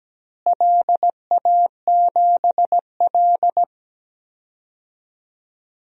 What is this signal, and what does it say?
Morse code, heard as a keyed tone, about 17 wpm, 700 Hz. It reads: LA7L
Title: DXCC-100 Call Signs - 4 Characters - Repeated 3x - 17wpm